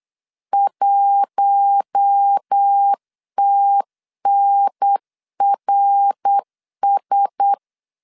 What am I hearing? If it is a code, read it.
Morse code, heard as a keyed tone, about 8 wpm, 800 Hz. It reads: 1TNRS
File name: LEZIONI / 8 YJ45 / 8 YJ 8wpm 3313